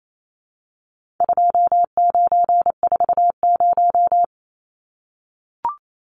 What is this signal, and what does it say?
Morse code, heard as a keyed tone, about 28 wpm, 700 Hz. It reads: 2940
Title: Numbers - 4-Digit 28wpm